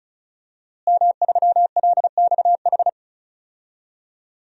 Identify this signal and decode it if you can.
Morse code, heard as a keyed tone, about 36 wpm, 700 Hz. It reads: M3LXH
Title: DXCC-100 Call Signs - 5 Characters - Repeated 2x - 35wpm